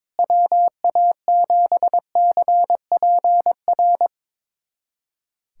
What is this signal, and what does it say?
Morse code, heard as a keyed tone, about 22 wpm, 700 Hz. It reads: WA7CPR
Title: USA Callsign DE Callsign 22wpm